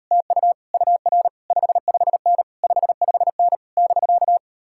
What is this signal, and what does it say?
Morse code, heard as a keyed tone, about 37 wpm, 700 Hz. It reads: TU UR 55N 55N <BK>